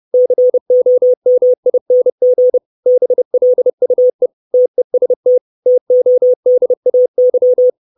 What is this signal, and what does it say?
Morse code, heard as a keyed tone, about 30 wpm, 500 Hz. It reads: COMING BLUE TEST TODAY
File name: 632 Words 30WPM